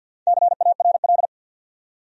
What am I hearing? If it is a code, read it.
Morse code, heard as a keyed tone, about 48 wpm, 700 Hz. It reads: CARL
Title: USA Common Names - Max 4 Letters - 50wpm